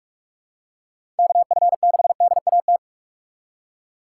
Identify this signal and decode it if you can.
Morse code, heard as a keyed tone, about 46 wpm, 700 Hz. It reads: KF6DAT
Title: USA Call Signs - 6 Characters - 2x Character Spacing - 45wpm